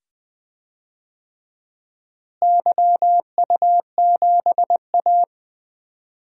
Morse code, heard as a keyed tone, about 20 wpm, 700 Hz. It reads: YU7A
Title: DXCC-100 Call Signs - 4 Characters - Repeated 2x - 20wpm